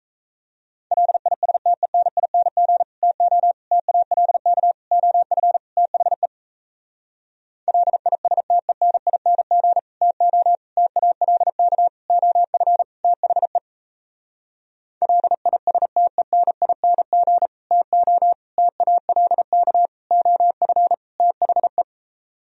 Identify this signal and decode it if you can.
Morse code, heard as a keyed tone, about 37 wpm, 700 Hz. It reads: LISTENING TO TALK OF THE LISTENING TO TALK OF THE LISTENING TO TALK OF THE